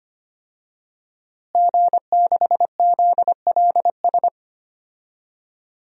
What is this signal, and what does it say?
Morse code, heard as a keyed tone, about 25 wpm, 700 Hz. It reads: G6ZLS
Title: DXCC-100 Call Signs - 5 Characters - Repeated 3x - 25wpm